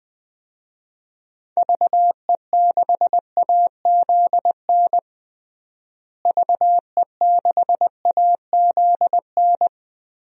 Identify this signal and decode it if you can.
Morse code, heard as a keyed tone, about 20 wpm, 700 Hz. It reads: VE6AZN VE6AZN
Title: DXCC-100 Call Signs - 6 Characters - Repeated 3x - 20wpm